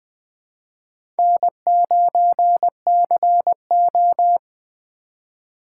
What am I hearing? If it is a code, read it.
Morse code, heard as a keyed tone, about 21 wpm, 700 Hz. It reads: N9CO
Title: USA Call Signs - 4 Characters - 2x Character Spacing - 20wpm